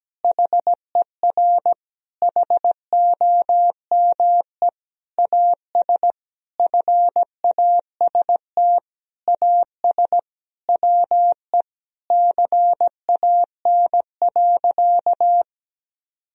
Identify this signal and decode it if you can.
Morse code, heard as a keyed tone, about 17 wpm, 700 Hz. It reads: HER HOME AS FAST AS WE CAN.
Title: Sentences from Top 1000 Words - Review Part 2 - 17wpm